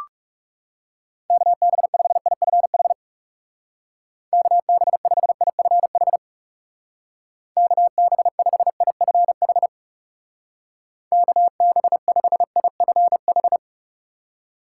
Morse code, heard as a keyed tone, about 36 wpm, 700 Hz. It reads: KB5IFH KB5IFH KB5IFH KB5IFH
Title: Speed Racing - USA 2x3 Call Signs 30wpm